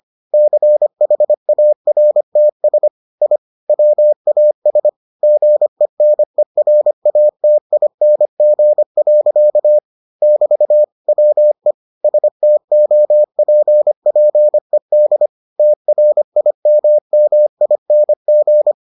Morse code, heard as a keyed tone, about 25 wpm, 600 Hz. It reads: CHARTS I WAS GENERATING. <BT> WE STOPPED TRIMMING